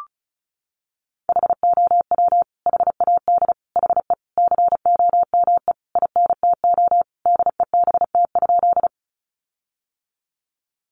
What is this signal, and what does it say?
Morse code, heard as a keyed tone, about 34 wpm, 700 Hz. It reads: HOW HAD HE COME INTO DEBT?
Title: Sentences from Top 1300 Words 35wpm